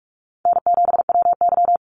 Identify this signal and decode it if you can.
Morse code, heard as a keyed tone, about 44 wpm, 700 Hz. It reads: N7WY
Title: Call Signs Repeated 2x - 45wpm